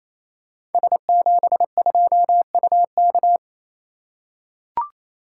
Morse code, heard as a keyed tone, about 27 wpm, 700 Hz. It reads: S72UK
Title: Code Groups - 5 Letter:Numbers - 28wpm